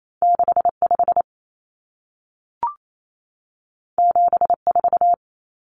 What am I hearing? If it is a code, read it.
Morse code, heard as a keyed tone, about 28 wpm, 700 Hz. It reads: 65 74
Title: Numbers - 2-Digit 28wpm